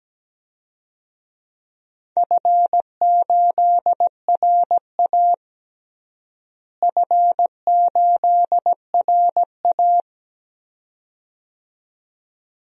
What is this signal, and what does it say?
Morse code, heard as a keyed tone, about 17 wpm, 700 Hz. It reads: F8RA F8RA
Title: DXCC-100 Call Signs - 4 Characters - Repeated 2x - 17wpm